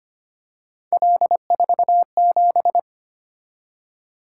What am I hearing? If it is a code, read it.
Morse code, heard as a keyed tone, about 25 wpm, 700 Hz. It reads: L47
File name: Three Letter-Number Combinations - 25wpm